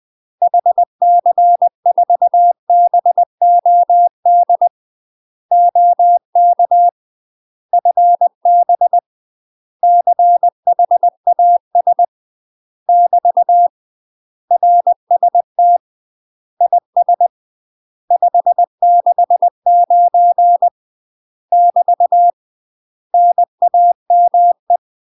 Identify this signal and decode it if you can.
Morse code, heard as a keyed tone, about 20 wpm, 700 Hz. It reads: HC4BOD OK FB CHAS <BT> RST IS 569 <BT> NAME
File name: UK MASTER QSOs TRAINING / 20WPM / UK MASTER QSOs 0050